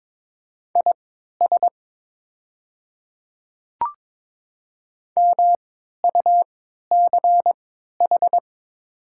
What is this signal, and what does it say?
Morse code, heard as a keyed tone, about 22 wpm, spaced as for 12 wpm, 700 Hz. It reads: IS MUCH
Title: 3x Character Spacing - Top 300 Words - 22wpm